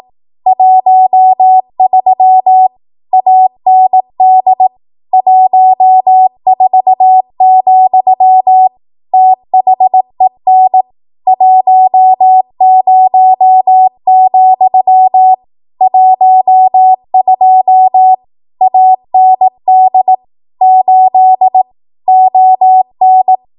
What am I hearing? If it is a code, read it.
Morse code, heard as a keyed tone, about 18 wpm, 750 Hz. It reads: 13 AND 14, THEN 10, 12 AND 8 ON
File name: Archive / 18 WPM / 18WPM